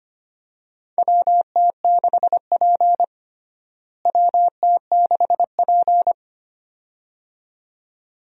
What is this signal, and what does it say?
Morse code, heard as a keyed tone, about 25 wpm, 700 Hz. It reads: WT6P WT6P
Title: USA Call Signs - 4 Characters - Repeated 2x - 25wpm